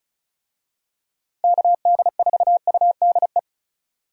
Morse code, heard as a keyed tone, about 34 wpm, 700 Hz. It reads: KD4UDE
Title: USA Call Signs - 6 Characters - 35wpm